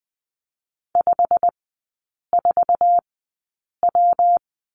Morse code, heard as a keyed tone, about 20 wpm, spaced as for 8 wpm, 700 Hz. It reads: 54W
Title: 114 Lesson 19e - Letter w - Three Characters 20wpm